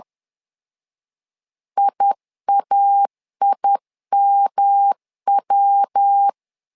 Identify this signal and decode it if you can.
Morse code, heard as a keyed tone, about 10 wpm, 800 Hz. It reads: IAIMW
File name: LEZIONI / 3 MWRO / 3 MWR 10wpm 1952